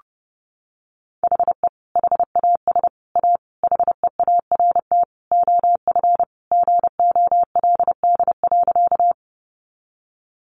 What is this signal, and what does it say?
Morse code, heard as a keyed tone, about 30 wpm, 700 Hz. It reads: HE HAS A HEART OF GOLD.
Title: Sentences from Top 1200 Words 30wpm